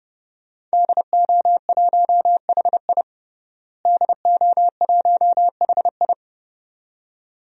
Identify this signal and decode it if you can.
Morse code, heard as a keyed tone, about 30 wpm, 700 Hz. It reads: DO1HI DO1HI
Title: DXCC-100 Call Signs - 5 Characters - Repeated 2x - 30wpm